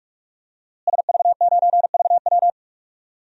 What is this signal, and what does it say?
Morse code, heard as a keyed tone, about 42 wpm, 700 Hz. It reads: SV9VW
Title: DXCC-100 Call Signs - 5 Characters - Repeated 3x - 45wpm